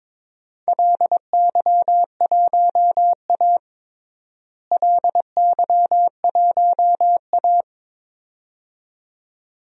Morse code, heard as a keyed tone, about 21 wpm, 700 Hz. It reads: LY1A LY1A
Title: DXCC-100 Call Signs - 4 Characters - Repeated 2x - 22wpm